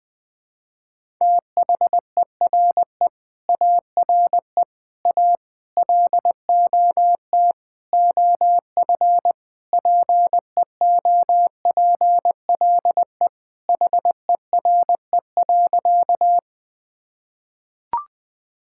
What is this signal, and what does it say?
Morse code, heard as a keyed tone, about 20 wpm, 700 Hz. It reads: THERE ARE A LOT OF PEOPLE HERE.
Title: Sentences with 6-Word Phrases - 20wpm